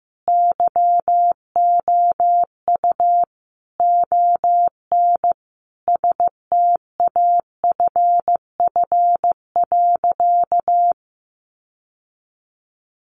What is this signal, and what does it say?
Morse code, heard as a keyed tone, about 15 wpm, 700 Hz. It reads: YOU ON STAFF.
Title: Sentences from Top 1200 Words 15wpm